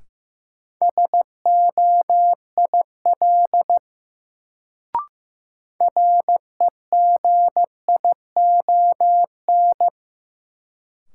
Morse code, heard as a keyed tone, about 15 wpm, 700 Hz. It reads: SOIL REGION
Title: Sentences from Top 1500 Words - New Word Drill Alt - 15wpm